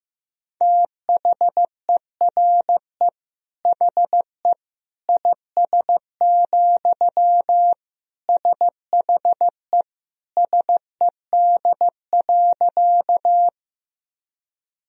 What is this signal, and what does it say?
Morse code, heard as a keyed tone, about 15 wpm, 700 Hz. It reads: THERE HE IS, SHE SED.
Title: Sentences from Top 100 Words and Abbreviations 15wpm